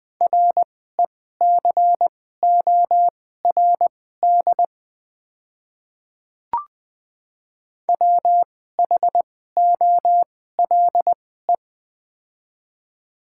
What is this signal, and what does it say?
Morse code, heard as a keyed tone, about 20 wpm, spaced as for 14 wpm, 700 Hz. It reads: RECORD WHOLE
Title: 2x Character Spacing - Top 500 Words - 20wpm